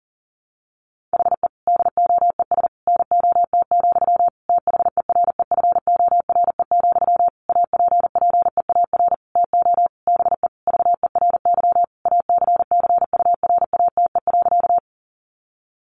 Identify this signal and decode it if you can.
Morse code, heard as a keyed tone, about 40 wpm, 700 Hz. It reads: HE DOES NOT, THEREFORE, APPEAR TO BE VERY ACCURATE.
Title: Sentences from Top 1200 Words 40wpm